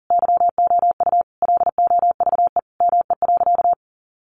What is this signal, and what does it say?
Morse code, heard as a keyed tone, about 40 wpm, 700 Hz. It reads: YOU LOVE ME.
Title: Sentences from Top 600 Words 40wpm